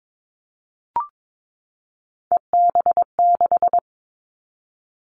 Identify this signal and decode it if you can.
Morse code, heard as a keyed tone, about 22 wpm, 700 Hz. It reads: EB6
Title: Callsign Prefixes for Top 50 DXCC - 22wpm